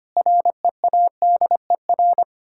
Morse code, heard as a keyed tone, about 25 wpm, 700 Hz. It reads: READER
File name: practice / 25wpm / Difficult Words with Hints - 25wpm